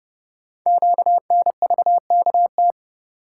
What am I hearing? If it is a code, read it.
Morse code, heard as a keyed tone, about 30 wpm, 700 Hz. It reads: QNVKT
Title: Code Groups - 5 Letter:Numbers - 30wpm